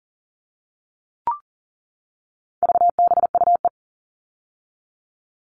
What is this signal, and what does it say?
Morse code, heard as a keyed tone, about 40 wpm, 700 Hz. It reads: VBUE
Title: All Cut Numbers - 4-Digit 40wpm